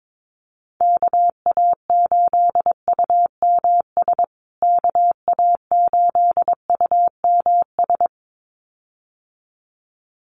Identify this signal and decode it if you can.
Morse code, heard as a keyed tone, about 22 wpm, 700 Hz. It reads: KA8UMS KA8UMS
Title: USA Call Signs Repeated 2x 22wpm